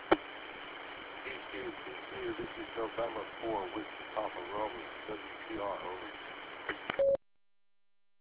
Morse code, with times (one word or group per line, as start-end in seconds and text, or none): none